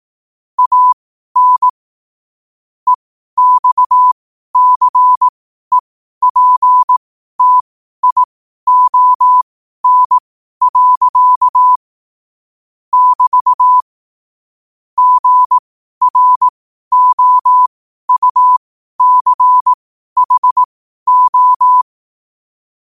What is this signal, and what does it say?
Morse code, heard as a keyed tone, about 18 wpm, spaced as for 12 wpm, 1000 Hz. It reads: AN EXCEPTION. <BT> GROUCHO
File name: qotdfiles / 12wpm